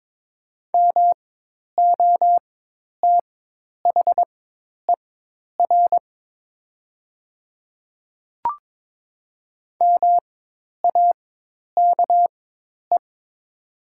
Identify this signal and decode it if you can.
Morse code, heard as a keyed tone, about 22 wpm, spaced as for 10 wpm, 700 Hz. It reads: MOTHER MAKE
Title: 4x Character Spacing - Top 200 Words - 22wpm